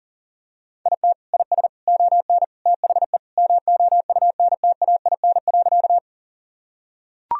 Morse code, heard as a keyed tone, about 40 wpm, 700 Hz. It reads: IT IS ON THE MOUNTAIN.
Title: Sentences from Top 1000 Words - Review Part 1 - 40wpm